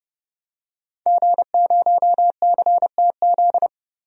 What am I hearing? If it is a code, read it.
Morse code, heard as a keyed tone, about 30 wpm, 700 Hz. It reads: G0CTZ
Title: DXCC-100 Call Signs - 5 Characters - 30wpm